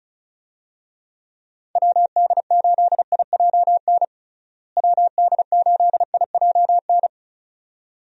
Morse code, heard as a keyed tone, about 34 wpm, 700 Hz. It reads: WD8IJN WD8IJN
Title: USA Call Signs - 6 Characters - Repeated 2x - 35wpm